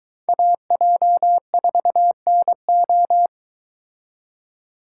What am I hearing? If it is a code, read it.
Morse code, heard as a keyed tone, about 22 wpm, 700 Hz. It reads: AJ4NO